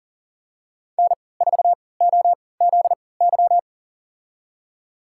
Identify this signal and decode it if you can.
Morse code, heard as a keyed tone, about 39 wpm, spaced as for 23 wpm, 700 Hz. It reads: N4OZY
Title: USA Call Signs - 5 Characters - 3x Character Spacing - 40wpm